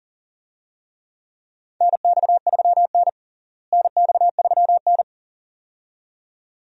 Morse code, heard as a keyed tone, about 37 wpm, 700 Hz. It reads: NX3N NX3N
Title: USA Call Signs - 4 Characters - Repeated 2x - 40wpm